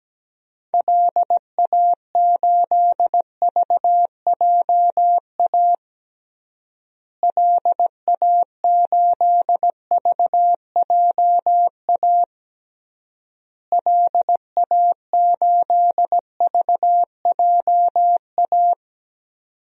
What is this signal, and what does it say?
Morse code, heard as a keyed tone, about 17 wpm, 700 Hz. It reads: LA8VJA LA8VJA LA8VJA
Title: DXCC-100 Call Signs - 6 Characters - Repeated 3x - 17wpm